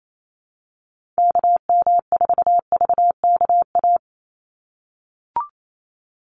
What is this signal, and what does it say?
Morse code, heard as a keyed tone, about 28 wpm, 700 Hz. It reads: KM4VKA
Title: USA Call Signs 28wpm